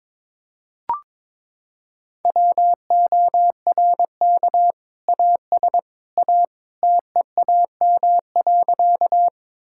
Morse code, heard as a keyed tone, about 22 wpm, 700 Hz. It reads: WORK AS A TEAM.
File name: practice / 22wpm / Sentences with 2-Word Phrases - 22wpm